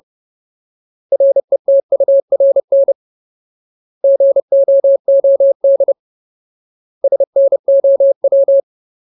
Morse code, heard as a keyed tone, about 30 wpm, 550 Hz. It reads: RETURN GOOD SNOW
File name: Ultimate Word List / 30WPM Ultimate